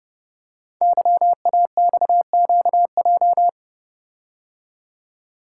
Morse code, heard as a keyed tone, about 29 wpm, 700 Hz. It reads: YAXQJ
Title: Code Groups - 5 Letters - 30wpm